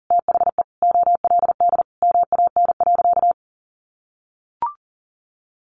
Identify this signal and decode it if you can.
Morse code, heard as a keyed tone, about 40 wpm, 700 Hz. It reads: THE OLD MAN.
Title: Sentences from Top 1100 Words 40wpm